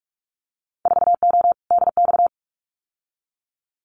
Morse code, heard as a keyed tone, about 46 wpm, 700 Hz. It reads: 4O DX
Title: Winter Field Day Exchanges Encore 45wpm